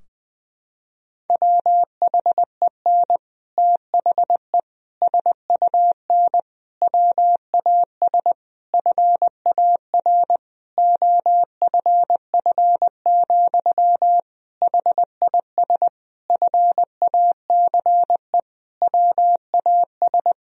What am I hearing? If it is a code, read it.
Morse code, heard as a keyed tone, about 20 wpm, 700 Hz. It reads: WHEN THE SUN WAS FAR OFF, HIS FACE WAS
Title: Sentences from Top 1000 Words - Review Part 3 - 20wpm